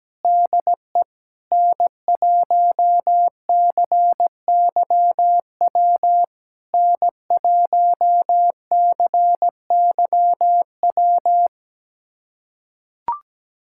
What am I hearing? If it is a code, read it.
Morse code, heard as a keyed tone, about 17 wpm, 700 Hz. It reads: DE N1CYW N1CYW
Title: SOTA QSO Practice 17wpm